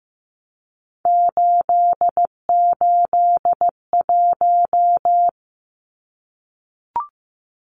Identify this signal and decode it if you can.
Morse code, heard as a keyed tone, about 15 wpm, 700 Hz. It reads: 881
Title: Numbers - 3-Digit 15wpm